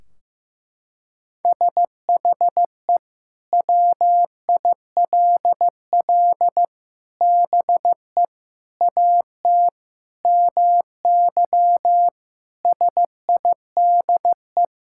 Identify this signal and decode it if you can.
Morse code, heard as a keyed tone, about 15 wpm, 700 Hz. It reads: SHE WILL BE AT MY SIDE
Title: Sentences from Top 1600 Words 15wpm